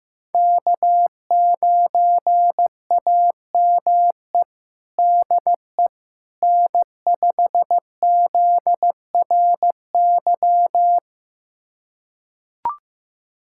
Morse code, heard as a keyed tone, about 15 wpm, 700 Hz. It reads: K9AME DE N5ZRY
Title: USA Callsign DE Callsign - 5 Characters - 15wpm